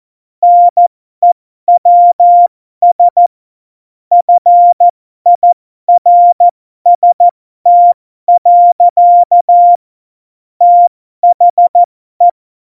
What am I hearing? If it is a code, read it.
Morse code, heard as a keyed tone, about 14 wpm, spaced as for 12 wpm, 700 Hz. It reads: NEWS FIRST. THE
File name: Morsekurs / Witze